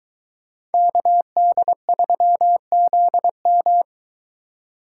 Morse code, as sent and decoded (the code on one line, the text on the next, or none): -.- -.. ...-- --.. --
KD3ZM